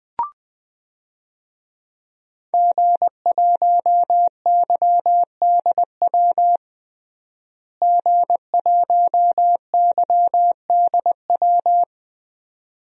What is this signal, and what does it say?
Morse code, heard as a keyed tone, about 20 wpm, 700 Hz. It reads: G1YDW G1YDW
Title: DXCC-100 Call Signs - 5 Characters - Repeated 2x - 20wpm